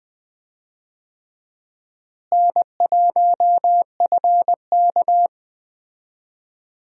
Morse code, heard as a keyed tone, about 20 wpm, 700 Hz. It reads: N1FK
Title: USA Call Signs - 4 Characters - Repeated 3x - 20wpm